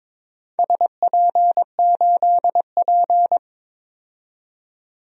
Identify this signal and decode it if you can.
Morse code, heard as a keyed tone, about 21 wpm, 700 Hz. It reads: SP8P
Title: DXCC-100 Call Signs - 4 Characters - 22wpm